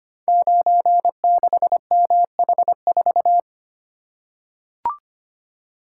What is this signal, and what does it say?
Morse code, heard as a keyed tone, about 25 wpm, 700 Hz. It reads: 96MH4
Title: Code Groups - 5 Letter:Numbers - 25wpm